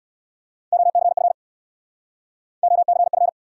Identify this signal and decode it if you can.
Morse code, heard as a keyed tone, about 75 wpm, 700 Hz. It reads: OCF OCF